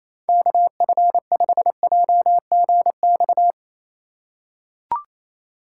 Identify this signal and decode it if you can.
Morse code, heard as a keyed tone, about 28 wpm, 700 Hz. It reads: KF5JGX
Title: USA Call Signs - 6 Characters - 2x Character Spacing - 28wpm